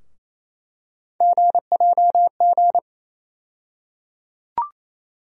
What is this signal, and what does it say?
Morse code, heard as a keyed tone, about 28 wpm, 700 Hz. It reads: GJG
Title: USA Call Sign Postfixes - 3 Letters 28wpm